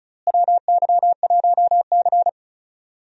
Morse code, heard as a keyed tone, about 34 wpm, 700 Hz. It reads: WY1C
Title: USA Call Signs - 4 Characters - Repeated 3x - 35wpm